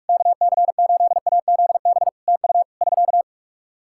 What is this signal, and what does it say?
Morse code, heard as a keyed tone, about 45 wpm, 700 Hz. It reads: KC8AZB TU <SK>
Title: Practice QSO - Exchange 4 - Ending - 45wpm